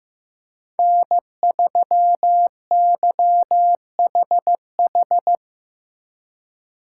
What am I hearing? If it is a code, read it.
Morse code, heard as a keyed tone, about 17 wpm, 700 Hz. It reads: N3YHH
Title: Speed Racing - USA 1x3 Call Signs 15wpm